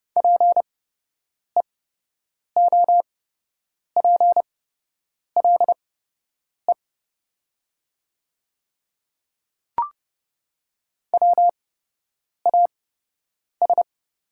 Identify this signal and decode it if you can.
Morse code, heard as a keyed tone, about 30 wpm, spaced as for 8 wpm, 700 Hz. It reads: PEOPLE WAS
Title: 8x Character Spacing - Top 100 Words - 30wpm